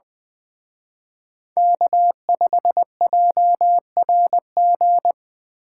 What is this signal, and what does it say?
Morse code, heard as a keyed tone, about 24 wpm, 700 Hz. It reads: K5JRG
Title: Speed Racing - USA 1x3 Call Signs 20wpm